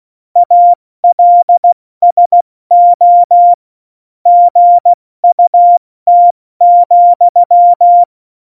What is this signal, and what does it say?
Morse code, heard as a keyed tone, about 16 wpm, 700 Hz. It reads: ALSO GUT,
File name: Morsekurs / Witze / Witz 26. Januar 2021, 70 F80